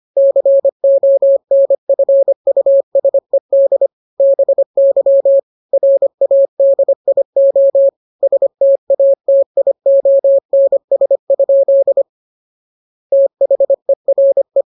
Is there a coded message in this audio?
Yes: Morse code from 0.2 to 14.6 s, heard as a keyed tone, about 25 wpm, 550 Hz. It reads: CONFUSED BY RADIO STATIONS? THERE